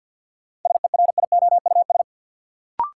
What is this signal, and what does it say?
Morse code, heard as a keyed tone, about 52 wpm, 700 Hz. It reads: SERIOUS
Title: Sentences from Top 1100 Words - New Word Drill - 50wpm